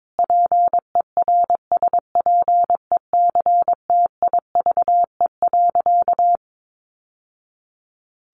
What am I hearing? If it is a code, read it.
Morse code, heard as a keyed tone, about 22 wpm, 700 Hz. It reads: PERSPECTIVE.
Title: Sentences from Top 1200 Words 22wpm